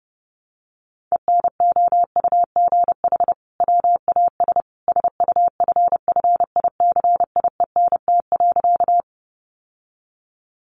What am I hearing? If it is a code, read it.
Morse code, heard as a keyed tone, about 30 wpm, 700 Hz. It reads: ENOUGH WAS SUFFICIENT.
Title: Sentences from Top 1400 Words 30wpm